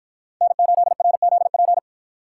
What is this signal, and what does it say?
Morse code, heard as a keyed tone, about 53 wpm, 700 Hz. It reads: N8RZP